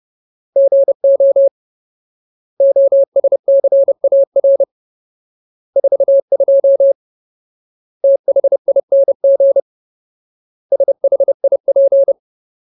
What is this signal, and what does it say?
Morse code, heard as a keyed tone, about 30 wpm, 550 Hz. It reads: GO OSCAR 42 THING SHIP
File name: Ultimate Word List / 30WPM Ultimate